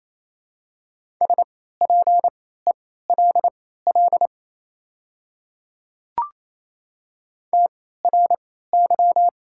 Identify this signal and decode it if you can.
Morse code, heard as a keyed tone, about 28 wpm, spaced as for 16 wpm, 700 Hz. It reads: SPELL TRY
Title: 3x Character Spacing - Top 500 Words - 28wpm